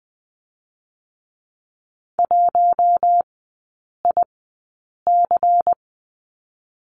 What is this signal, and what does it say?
Morse code, heard as a keyed tone, about 20 wpm, 700 Hz. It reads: 1 I C
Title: 218 Lesson 35e - Number 0 - Three Characters 20wpm